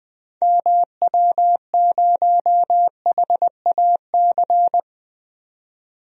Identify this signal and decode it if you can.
Morse code, heard as a keyed tone, about 20 wpm, 700 Hz. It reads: MW0HAC
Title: DXCC-100 Call Signs - 6 Characters - 20wpm